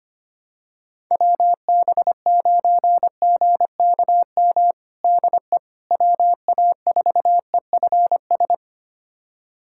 Morse code, heard as a keyed tone, about 25 wpm, 700 Hz. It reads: WB9GKM DE WA4EFS
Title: USA Callsign DE Callsign - 6 Characters - 25wpm